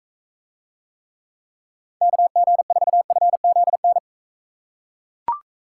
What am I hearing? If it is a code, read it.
Morse code, heard as a keyed tone, about 42 wpm, 700 Hz. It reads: KG4FZN